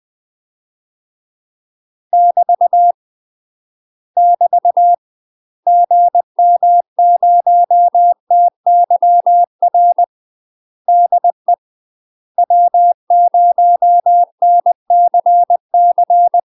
Morse code, heard as a keyed tone, about 20 wpm, 700 Hz. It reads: <BT> <BT> GM0TYR DE W0NCC
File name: UK MASTER QSOs TRAINING / 20WPM / UK MASTER QSOs 0024